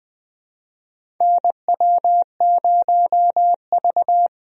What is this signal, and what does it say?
Morse code, heard as a keyed tone, about 20 wpm, 700 Hz. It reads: NW0V
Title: USA Call Signs - 4 Characters - Repeated 2x - 20wpm